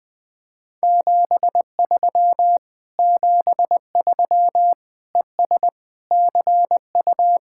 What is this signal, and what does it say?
Morse code, heard as a keyed tone, about 20 wpm, 700 Hz. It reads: 73 73 ES CU
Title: Speed-Racing Common QSO Phrases - 17wpm